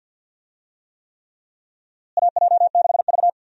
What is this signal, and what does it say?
Morse code, heard as a keyed tone, about 46 wpm, 700 Hz. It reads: AJ6V
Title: USA Call Signs - 4 Characters - Repeated 3x - 50wpm